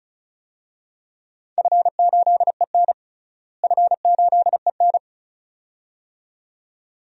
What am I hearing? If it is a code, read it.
Morse code, heard as a keyed tone, about 36 wpm, 700 Hz. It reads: F8EN F8EN
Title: DXCC-100 Call Signs - 4 Characters - Repeated 2x - 35wpm